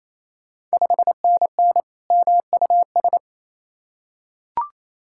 Morse code, heard as a keyed tone, about 28 wpm, 700 Hz. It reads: 5NN MUS
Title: Ontario QSO Party Exchange 28wpm